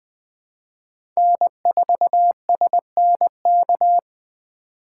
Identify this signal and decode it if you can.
Morse code, heard as a keyed tone, about 20 wpm, 700 Hz. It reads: N4SNK
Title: USA Call Signs - 5 Characters - 4x Character Spacing - 20wpm